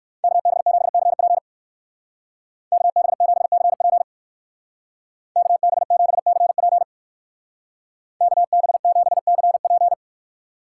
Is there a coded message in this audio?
Yes: Morse code from 0.2 to 9.9 s, heard as a keyed tone, about 57 wpm, 690 Hz. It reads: KB7CP KB7CP KB7CP KB7CP